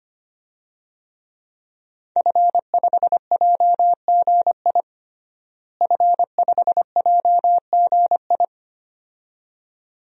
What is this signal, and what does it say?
Morse code, heard as a keyed tone, about 25 wpm, 700 Hz. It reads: F5JGI F5JGI
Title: DXCC-100 Call Signs - 5 Characters - Repeated 2x - 25wpm